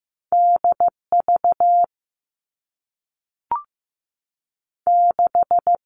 Morse code, heard as a keyed tone, about 15 wpm, 700 Hz. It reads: DV 6
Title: All Cut Numbers - 2-Digit 15wpm